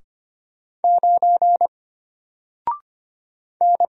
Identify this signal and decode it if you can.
Morse code, heard as a keyed tone, about 25 wpm, 700 Hz. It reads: 9 N
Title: Single Letter-Number Alt 25wpm